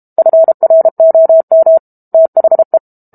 Morse code, heard as a keyed tone, about 32 wpm, 650 Hz. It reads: FROM THE
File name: Book Two / 06-Chapter 6 - 32 WPM